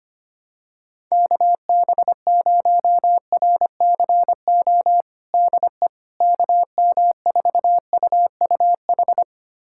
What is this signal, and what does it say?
Morse code, heard as a keyed tone, about 25 wpm, 700 Hz. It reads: KB0RCO DE KM4UUH
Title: USA Callsign DE Callsign - 6 Characters - 25wpm